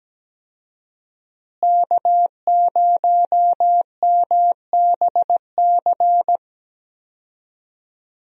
Morse code, heard as a keyed tone, about 17 wpm, 700 Hz. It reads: K0MBC
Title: USA Call Signs - 5 Characters - Repeated 3x - 17wpm